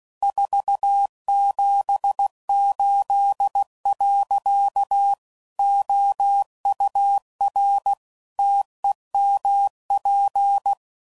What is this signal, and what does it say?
Morse code, heard as a keyed tone, about 16 wpm, 800 Hz. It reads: 478. OUR TEMP
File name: CW Code Groups @15WPM